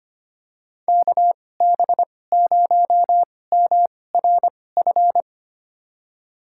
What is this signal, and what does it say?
Morse code, heard as a keyed tone, about 25 wpm, spaced as for 18 wpm, 700 Hz. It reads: KB0MRF
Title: USA Call Signs - 2x Character Spacing - 25wpm